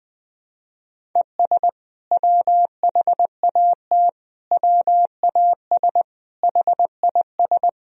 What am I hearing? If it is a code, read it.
Morse code, heard as a keyed tone, about 20 wpm, 700 Hz. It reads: ES WHAT WAS HIS
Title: Sentences from Top 200 Words and Abbreviations 20wpm